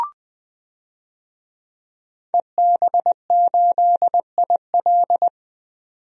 Morse code, heard as a keyed tone, about 20 wpm, 700 Hz. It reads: EB8IL
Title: DXCC-100 Call Signs - 5 Characters - 20wpm